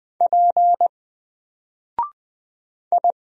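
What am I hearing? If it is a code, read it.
Morse code, heard as a keyed tone, about 20 wpm, 700 Hz. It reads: P I
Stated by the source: Single letter and number Morse drill at 20 wpm, from N4LCD.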